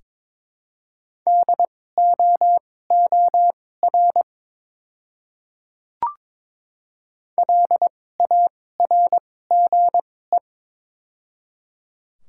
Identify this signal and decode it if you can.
Morse code, heard as a keyed tone, about 22 wpm, spaced as for 16 wpm, 700 Hz. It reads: DOOR LARGE
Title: 2x Character Spacing - Top 300 Words - 22wpm